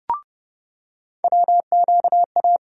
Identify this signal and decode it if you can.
Morse code, heard as a keyed tone, about 29 wpm, 700 Hz. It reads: WQA
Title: USA Call Sign Postfixes - 3 Letters 30wpm